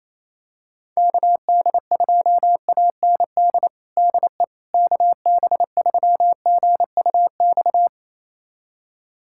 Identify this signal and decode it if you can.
Morse code, heard as a keyed tone, about 27 wpm, 700 Hz. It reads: KD2AND DE KB3GUX
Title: USA Callsign DE Callsign - 6 Characters - 28wpm